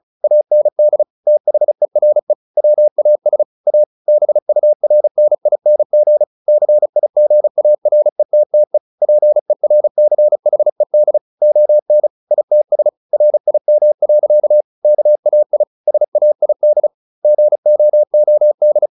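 Morse code, heard as a keyed tone, about 34 wpm, 600 Hz. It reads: AND THERE WAS A BURNING CIGARETTE PERCHED ON ITS RIM. KAI SAID GOOD
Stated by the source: Morse practice at 35 wpm from CW med Gnister, sending Ken Follett's novel Never in English.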